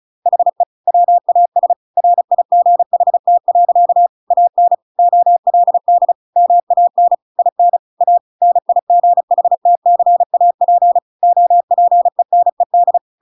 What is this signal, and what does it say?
Morse code, heard as a keyed tone, about 34 wpm, 700 Hz. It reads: HE WAS RIGHT. AN OLD MAN IN A NIGHTCAP OPENED